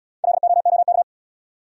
Morse code, heard as a keyed tone, about 76 wpm, 700 Hz. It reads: LOOK